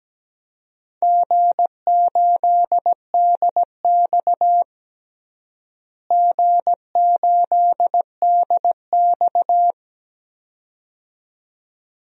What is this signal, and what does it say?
Morse code, heard as a keyed tone, about 17 wpm, 700 Hz. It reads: G8DX G8DX
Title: DXCC-100 Call Signs - 4 Characters - Repeated 2x - 17wpm